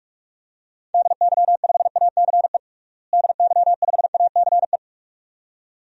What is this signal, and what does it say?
Morse code, heard as a keyed tone, about 42 wpm, 690 Hz. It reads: DY5ACE DY5ACE
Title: DXCC-100 Call Signs - 6 Characters - Repeated 2x - 45wpm